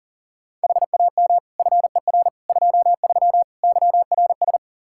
Morse code, heard as a keyed tone, about 40 wpm, 690 Hz. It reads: HAM FER 23 YRS
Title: Practice QSO - Exchange 3 - More Info - 40wpm